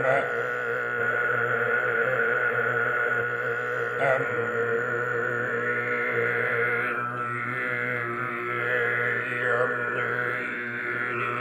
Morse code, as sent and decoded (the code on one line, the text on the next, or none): none